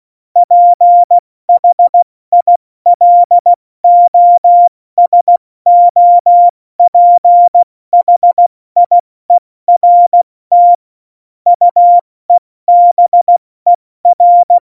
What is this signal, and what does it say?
Morse code, heard as a keyed tone, about 16 wpm, 700 Hz. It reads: PHILOSOPHIERT UEBER
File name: Morsekurs / Witze / Witz 08. Dezember 2020, 70 F80